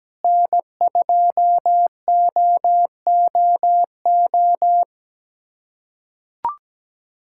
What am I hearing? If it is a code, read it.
Morse code, heard as a keyed tone, about 17 wpm, 700 Hz. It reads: N2OOO
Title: USA Callsign DE Callsign - 5 Characters - 17wpm